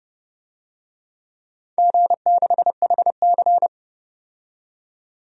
Morse code, heard as a keyed tone, about 30 wpm, 700 Hz. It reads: G6HC
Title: DXCC-100 Call Signs - 4 Characters - 30wpm